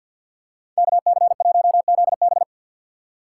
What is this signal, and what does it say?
Morse code, heard as a keyed tone, about 53 wpm, 690 Hz. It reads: KC1ZB
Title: USA Call Signs - 5 Characters - 4x Character Spacing - 50wpm